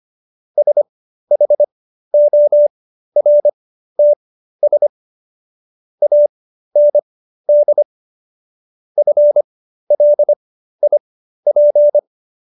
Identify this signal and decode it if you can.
Morse code, heard as a keyed tone, about 25 wpm, spaced as for 13 wpm, 590 Hz. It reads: SHORTS AND FLIP